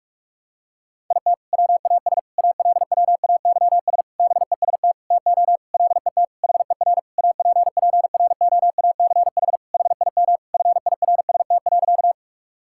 Morse code, heard as a keyed tone, about 47 wpm, 700 Hz. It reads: IT WAS ALWAYS BEST TO LET HER APPROACH HIM FIRST.
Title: Sentences from Top 1000 Words - Review Part 6 - 45wpm